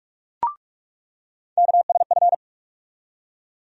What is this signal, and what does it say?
Morse code, heard as a keyed tone, about 42 wpm, 700 Hz. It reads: KSF
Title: USA Call Sign Postfixes - 3 Letters 45wpm